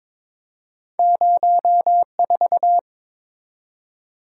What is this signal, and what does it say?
Morse code, heard as a keyed tone, about 22 wpm, 700 Hz. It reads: 04
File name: practice / 22wpm / Two Letter-Number Combinations - 22wpm